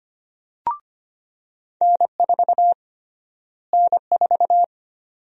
Morse code, heard as a keyed tone, about 25 wpm, 700 Hz. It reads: N4 N4
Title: USA Call Sign Prefixes - Repeated 2x - 25wpm